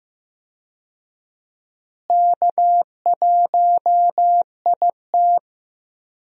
Morse code, heard as a keyed tone, about 15 wpm, 700 Hz. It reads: K1IT
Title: USA Call Signs - 4 Characters - Repeated 2x - 15wpm